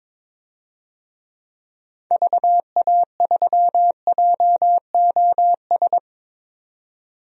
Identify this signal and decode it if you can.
Morse code, heard as a keyed tone, about 22 wpm, 700 Hz. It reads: VA3JOS
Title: DXCC-100 Call Signs - 6 Characters - 22wpm